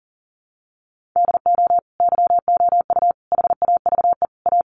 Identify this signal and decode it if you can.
Morse code, heard as a keyed tone, about 40 wpm, 700 Hz. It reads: DO YOU HAVE A
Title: Sentences from Top 1500 Words 40wpm